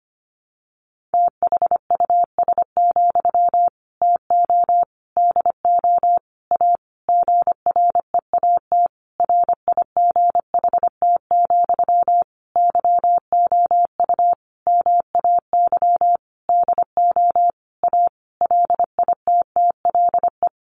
Morse code, heard as a keyed tone, about 25 wpm, 700 Hz. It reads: THUS, TO DO A GREAT RIGHT, YOU MAY DO A LITTLE
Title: Sentences from Top 900 Words 25wpm